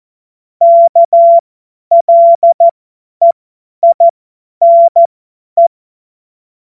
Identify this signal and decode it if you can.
Morse code, heard as a keyed tone, about 14 wpm, spaced as for 10 wpm, 680 Hz. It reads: KLEINE
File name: Morsekurs / Witze / Witz 25. Oktober 2022 50 F70